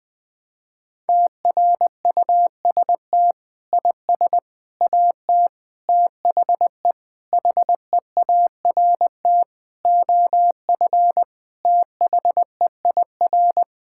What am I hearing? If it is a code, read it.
Morse code, heard as a keyed tone, about 20 wpm, 700 Hz. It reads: TRUST IS AT THE HEART OF THEIR
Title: Sentences with 5-Word Phrases - Spoken Beginning - 20wpm